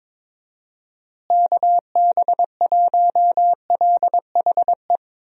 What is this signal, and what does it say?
Morse code, heard as a keyed tone, about 22 wpm, 700 Hz. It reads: KB1LHE
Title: USA Call Signs - 6 Characters - Repeated 2x - 22wpm